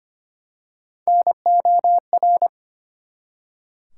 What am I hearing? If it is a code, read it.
Morse code, heard as a keyed tone, about 25 wpm, 700 Hz. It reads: NOR
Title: Three Letter Words - 25wpm